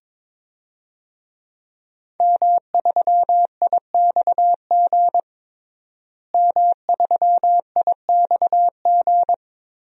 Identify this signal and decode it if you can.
Morse code, heard as a keyed tone, about 22 wpm, 700 Hz. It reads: M3IXG M3IXG
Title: DXCC-100 Call Signs - 5 Characters - Repeated 3x - 22wpm